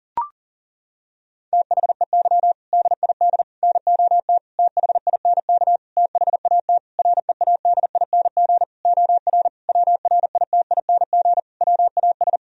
Morse code, heard as a keyed tone, about 40 wpm, 700 Hz. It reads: THEY DID NOT THINK THAT READING OR WRITING WAS
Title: Sentences from Top 1000 Words - Review Part 3 - 40wpm